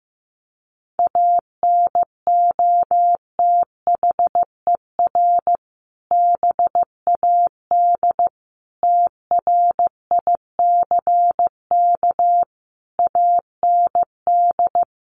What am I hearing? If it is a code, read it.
Morse code, heard as a keyed tone, about 14 wpm, 700 Hz. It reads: ANOTHER BAD TRICK AND
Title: Sentences from Top 1500 Words 15wpm